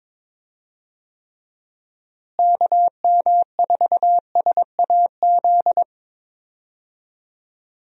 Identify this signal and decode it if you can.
Morse code, heard as a keyed tone, about 22 wpm, 700 Hz. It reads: KM4SAZ
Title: USA Call Signs - 6 Characters - Repeated 2x - 22wpm